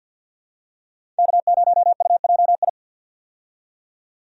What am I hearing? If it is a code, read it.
Morse code, heard as a keyed tone, about 51 wpm, 700 Hz. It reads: K0UJI
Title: USA Call Signs - 5 Characters - 50wpm